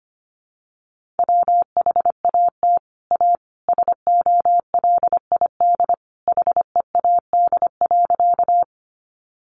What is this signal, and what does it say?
Morse code, heard as a keyed tone, about 25 wpm, 700 Hz. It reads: WHAT A SOLID HEAD.
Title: Sentences from Top 1500 Words 25wpm